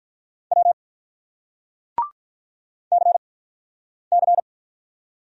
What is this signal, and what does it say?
Morse code, heard as a keyed tone, about 50 wpm, spaced as for 9 wpm, 700 Hz. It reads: W CC